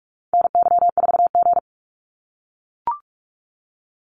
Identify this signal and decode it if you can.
Morse code, heard as a keyed tone, about 44 wpm, 700 Hz. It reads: NY4G
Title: NA SOTA Activator Callsigns 45wpm